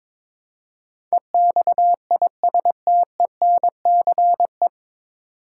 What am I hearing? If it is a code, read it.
Morse code, heard as a keyed tone, about 22 wpm, 700 Hz. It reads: EXISTENCE
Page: Morse Code Ninja practice 22wpm